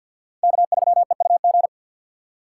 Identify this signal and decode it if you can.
Morse code, heard as a keyed tone, about 48 wpm, 700 Hz. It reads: K3EUG
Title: USA Call Signs - 5 Characters - 6x Character Spacing - 50wpm